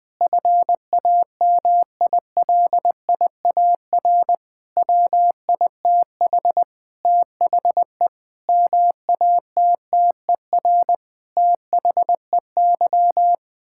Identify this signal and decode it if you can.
Morse code, heard as a keyed tone, about 20 wpm, 700 Hz. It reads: FAMILIAR WITH THE MATTER THEY
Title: Sentences with 7-Word Phrases - 20wpm